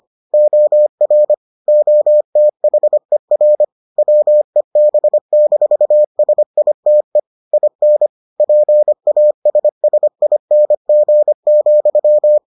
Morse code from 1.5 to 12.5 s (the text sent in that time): OTHER WEB-SITE IN PASSING,